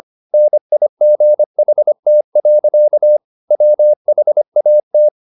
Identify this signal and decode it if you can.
Morse code, heard as a keyed tone, about 25 wpm, 600 Hz. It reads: NIGHT. WHAT
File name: the old man and the sea part 1 25wpm